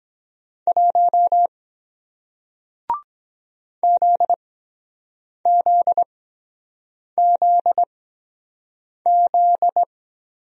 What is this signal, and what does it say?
Morse code, heard as a keyed tone, about 24 wpm, spaced as for 7 wpm, 700 Hz. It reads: 1 ZZZZ